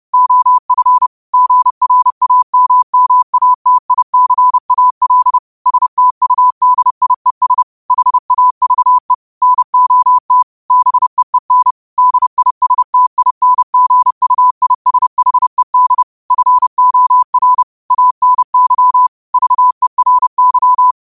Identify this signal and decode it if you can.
Morse code, heard as a keyed tone, about 30 wpm, 1000 Hz. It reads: OF GRAMMATICAL STUDIES HAVE NOT BEEN DISTINGUISHED FOR ANY VERY